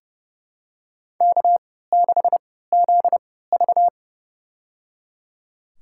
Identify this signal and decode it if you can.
Morse code, heard as a keyed tone, about 31 wpm, spaced as for 17 wpm, 700 Hz. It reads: K6ZV
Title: USA Call Signs - 3x Character Spacing - 30wpm